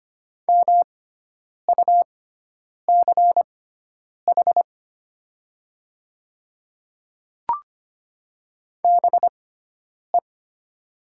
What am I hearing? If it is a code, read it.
Morse code, heard as a keyed tone, about 25 wpm, spaced as for 9 wpm, 700 Hz. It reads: MUCH BE
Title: 6x Character Spacing - Top 200 Words - 25wpm